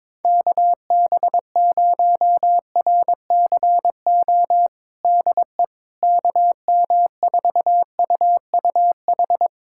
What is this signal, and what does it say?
Morse code, heard as a keyed tone, about 22 wpm, 700 Hz. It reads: KB0RCO DE KM4UUH
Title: USA Callsign DE Callsign - 6 Characters - 22wpm